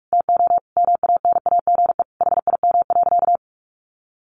Morse code, heard as a keyed tone, about 46 wpm, 700 Hz. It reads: TO MANAGE HIM.
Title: Sentences from Top 1400 Words 45wpm